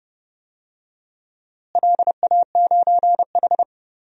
Morse code, heard as a keyed tone, about 30 wpm, 700 Hz. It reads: LA9H
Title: DXCC-100 Call Signs - 4 Characters - Repeated 2x - 30wpm